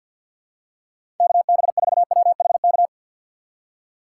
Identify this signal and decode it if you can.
Morse code, heard as a keyed tone, about 47 wpm, 700 Hz. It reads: K B 4 W H K
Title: USA Call Signs - 6 Characters - 6x Character Spacing - 50wpm